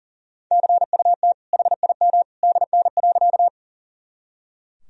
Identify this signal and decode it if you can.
Morse code, heard as a keyed tone, about 40 wpm, 700 Hz. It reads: CUT HIM DN.
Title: Sentences from Top 300 Words and Abbreviations 40wpm